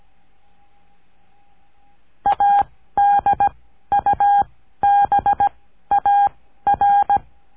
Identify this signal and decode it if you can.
Morse code, heard as a keyed tone, about 17 wpm, spaced as for 12 wpm, 810 Hz. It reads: ADUBAR
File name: cursos / CURSO DE TELEGRAFIA / Telegrafia - Trilha 14